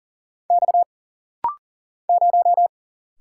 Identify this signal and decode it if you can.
Morse code, heard as a keyed tone, about 40 wpm, 700 Hz. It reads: X 0
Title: Single Letter-Number - Rapid-Fire 40wpm